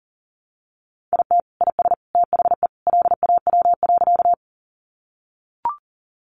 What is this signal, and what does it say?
Morse code, heard as a keyed tone, about 40 wpm, 700 Hz. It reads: IT IS THE LAW.
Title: Sentences from Top 600 Words 40wpm